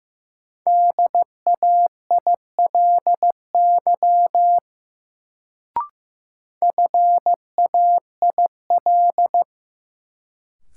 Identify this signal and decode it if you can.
Morse code, heard as a keyed tone, about 15 wpm, 700 Hz. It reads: DAILY FAIL
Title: Sentences from Top 1100 Words - New Word Drill Alt - 15wpm